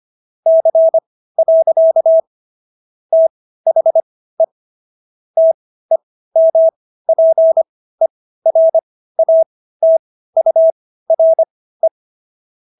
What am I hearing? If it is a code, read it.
Morse code, heard as a keyed tone, about 25 wpm, spaced as for 15 wpm, 650 Hz. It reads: C. THE TEMPERATURE